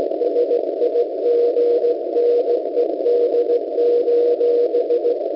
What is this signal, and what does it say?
Morse code, heard as a keyed tone, about 17 wpm, 480 Hz, with noise. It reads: SIGNED: